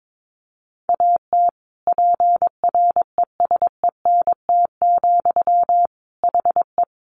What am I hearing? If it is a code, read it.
Morse code, heard as a keyed tone, about 22 wpm, 700 Hz. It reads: AT PRESENT, HE
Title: Sentences from Top 800 Words 22wpm